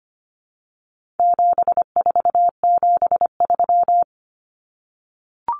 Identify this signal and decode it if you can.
Morse code, heard as a keyed tone, about 25 wpm, 700 Hz. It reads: 7473
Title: Common Cut Numbers - 4-Digit 25wpm